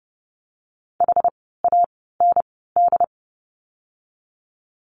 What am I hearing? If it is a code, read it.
Morse code, heard as a keyed tone, about 30 wpm, spaced as for 17 wpm, 700 Hz. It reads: HAND
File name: course / 30:10wpm / 108 Lesson 18f - Letter m - Words 30wpm